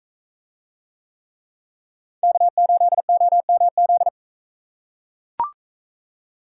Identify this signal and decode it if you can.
Morse code, heard as a keyed tone, about 42 wpm, 700 Hz. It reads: K8OMZ